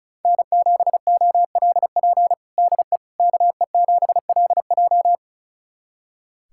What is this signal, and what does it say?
Morse code, heard as a keyed tone, about 36 wpm, 700 Hz. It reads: N7OLP DE KE7LJ
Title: USA Callsign DE Callsign - 5 Characters - 35wpm